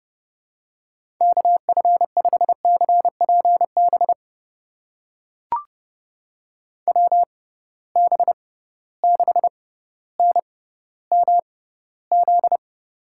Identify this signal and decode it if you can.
Morse code, heard as a keyed tone, about 32 wpm, spaced as for 16 wpm, 700 Hz. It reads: KF5CPB WB6NMZ
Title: USA Call Signs - 6 Characters - 6x Character Spacing - 30wpm